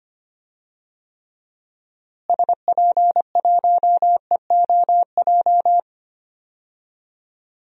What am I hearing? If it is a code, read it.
Morse code, heard as a keyed tone, about 25 wpm, 700 Hz. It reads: SP1EOJ
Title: DXCC-100 Call Signs - 6 Characters - Repeated 3x - 25wpm